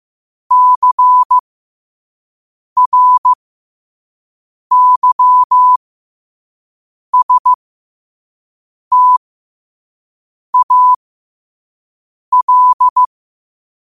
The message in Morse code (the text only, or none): CRYSTAL